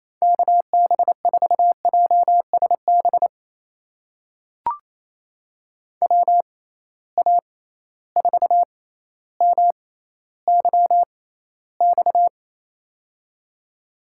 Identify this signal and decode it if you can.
Morse code, heard as a keyed tone, about 27 wpm, spaced as for 14 wpm, 700 Hz. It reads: KB4JSB WA4MYX
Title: USA Call Signs - 6 Characters - 6x Character Spacing - 28wpm